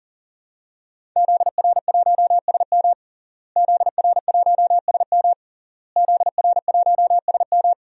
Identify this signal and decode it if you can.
Morse code, heard as a keyed tone, about 40 wpm, 700 Hz. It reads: ZR1SM ZR1SM ZR1SM
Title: DXCC-100 Call Signs - 5 Characters - Repeated 3x - 40wpm